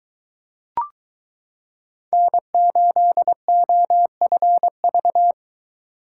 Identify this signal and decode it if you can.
Morse code, heard as a keyed tone, about 22 wpm, 700 Hz. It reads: N8OFV